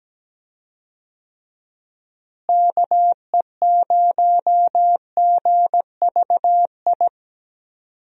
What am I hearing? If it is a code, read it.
Morse code, heard as a keyed tone, about 17 wpm, 700 Hz. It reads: KE0GVI
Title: USA Call Signs - 6 Characters - Repeated 2x - 17wpm